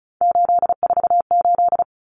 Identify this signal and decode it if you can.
Morse code, heard as a keyed tone, about 34 wpm, 700 Hz. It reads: 848
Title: Numbers - 3-Digit 35wpm